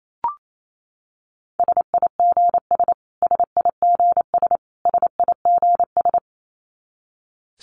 Morse code, heard as a keyed tone, about 28 wpm, 700 Hz. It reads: SIGS SIGS SIGS
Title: Abbreviations Repeated 3x - 28wpm